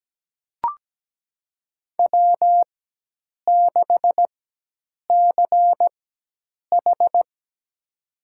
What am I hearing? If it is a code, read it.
Morse code, heard as a keyed tone, about 17 wpm, spaced as for 8 wpm, 700 Hz. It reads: W6CH
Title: USA Call Signs - 4 Characters - 4x Character Spacing - 17wpm